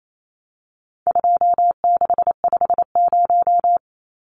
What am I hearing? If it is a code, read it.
Morse code, heard as a keyed tone, about 28 wpm, 700 Hz. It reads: 2650
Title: Numbers - 4-Digit 28wpm